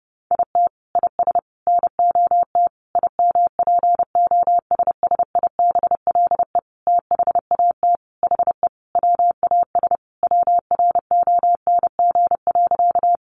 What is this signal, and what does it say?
Morse code, heard as a keyed tone, about 30 wpm, 700 Hz. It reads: IT IS NOT IMPOSSIBLE THAT HE WAS WRONG.
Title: Sentences from Top 1100 Words 30wpm